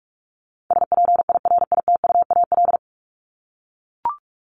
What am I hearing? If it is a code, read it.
Morse code, heard as a keyed tone, about 44 wpm, 700 Hz. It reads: SPIRITUAL
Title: Top 1501-1750 Words 45wpm